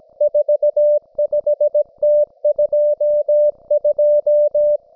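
Morse code, heard as a keyed tone, about 17 wpm, 590 Hz. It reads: 45T22
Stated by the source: Russian military Morse station, M32, RDL, on 5268 kHz.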